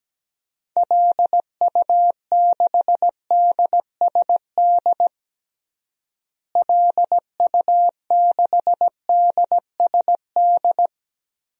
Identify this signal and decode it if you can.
Morse code, heard as a keyed tone, about 17 wpm, 700 Hz. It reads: LU6DSD LU6DSD
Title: DXCC-100 Call Signs - 6 Characters - Repeated 3x - 17wpm